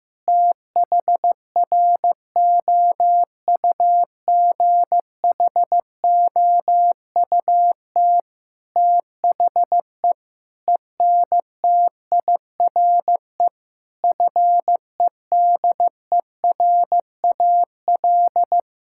Morse code, heard as a keyed tone, about 16 wpm, 700 Hz. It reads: THROUGHOUT THE ENTIRE FEDERAL
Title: Sentences from Top 1000 Words - Review Part 5 - 15wpm